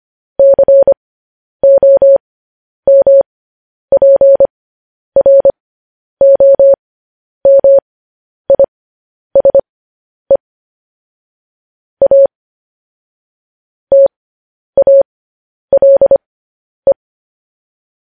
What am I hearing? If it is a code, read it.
Morse code, heard as a keyed tone, about 25 wpm, spaced as for 10 wpm, 550 Hz. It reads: COMPROMISE A TALE